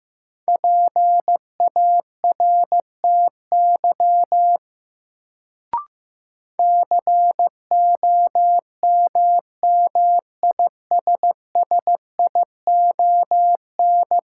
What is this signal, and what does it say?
Morse code, heard as a keyed tone, about 15 wpm, 700 Hz. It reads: PARTY COMMISSION
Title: Sentences from Top 1400 Words - New Word Drill - 15wpm